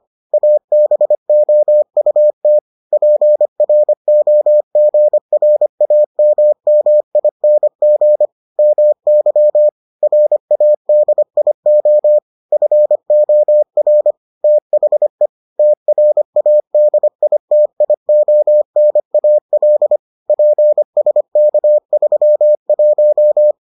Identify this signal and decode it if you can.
Morse code, heard as a keyed tone, about 25 wpm, 600 Hz. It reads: ABOUT PROGRAMMING MY RADIO FOR THE TRADITIONAL PSK31